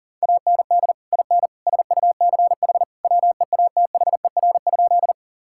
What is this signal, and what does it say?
Morse code, heard as a keyed tone, about 40 wpm, 700 Hz. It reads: AND IN SUCH WEATHER?
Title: Sentences from Top 1000 Words - Review Part 5 - 40wpm